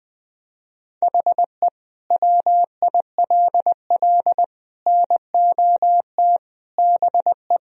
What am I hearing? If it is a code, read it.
Morse code, heard as a keyed tone, about 20 wpm, 700 Hz. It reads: HE WILL NOT BE
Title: Sentences from Top 1000 Words - Review Part 4 - 20wpm